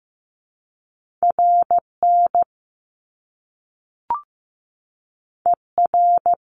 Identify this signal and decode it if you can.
Morse code, heard as a keyed tone, about 15 wpm, 700 Hz. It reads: RN ER
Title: Top Two Letter Combinations in Words 15wpm